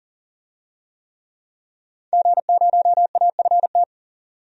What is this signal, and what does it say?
Morse code, heard as a keyed tone, about 40 wpm, 700 Hz. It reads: G0AFT
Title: DXCC-100 Call Signs - 5 Characters - 40wpm